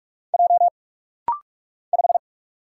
Morse code, heard as a keyed tone, about 46 wpm, 700 Hz. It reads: J 5
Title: Single Letter-Number - Rapid-Fire 45wpm